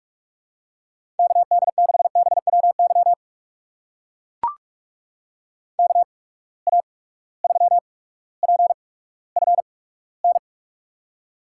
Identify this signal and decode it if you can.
Morse code, heard as a keyed tone, about 44 wpm, spaced as for 19 wpm, 700 Hz. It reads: KD6BWY KA3PFN
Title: USA Call Signs - 6 Characters - 8x Character Spacing - 45wpm